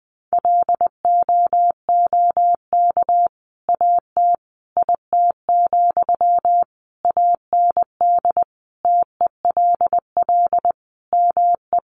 Morse code, heard as a keyed tone, about 20 wpm, 700 Hz. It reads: LOOK AT IT, AND TELL ME
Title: Sentences from Top 200 Words 20wpm